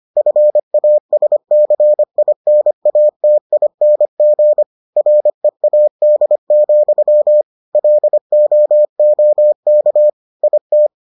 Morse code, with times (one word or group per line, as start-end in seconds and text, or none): 0.2-4.6 FASCINATING
5.0-7.4 READ,
7.8-10.1 LOOK
10.4-10.9 IT